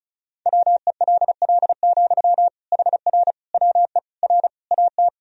Morse code, heard as a keyed tone, about 36 wpm, 700 Hz. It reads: WELL, HR WE R AT
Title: Sentences from Top 300 Words and Abbreviations 35wpm